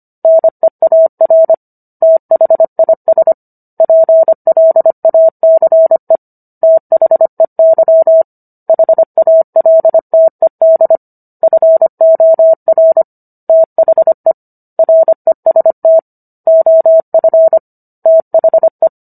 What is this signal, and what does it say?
Morse code, heard as a keyed tone, about 25 wpm, 650 Hz. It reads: NEAR THIS PLACE THEY HALTED FOR THE REST OF THE
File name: Book One / 16-Chapter 16 - 25 WPM